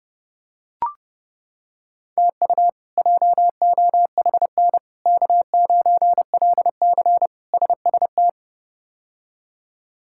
Morse code, heard as a keyed tone, about 30 wpm, 700 Hz. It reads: TU JOHN K9LC SST
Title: K1USN SST QSO Practice - 30wpm